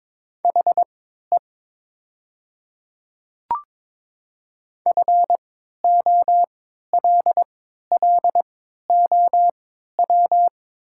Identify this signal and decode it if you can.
Morse code, heard as a keyed tone, about 22 wpm, spaced as for 12 wpm, 700 Hz. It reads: HE FOLLOW
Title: 3x Character Spacing - Top 300 Words - 22wpm